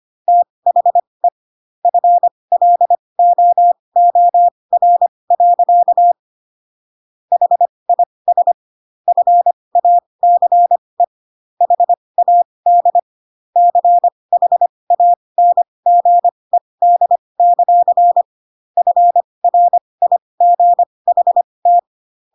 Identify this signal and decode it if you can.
Morse code, heard as a keyed tone, about 25 wpm, spaced as for 20 wpm, 700 Hz. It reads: THE FLOOR. HIS FACE HAD CHANGED; FRIGHT
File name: Pinocchio / Pinocchio ch1 15 25 ik1hge